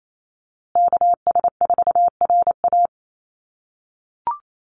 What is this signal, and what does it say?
Morse code, heard as a keyed tone, about 28 wpm, 700 Hz. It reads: KS4RA
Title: USA Call Signs Repeated 2x 28wpm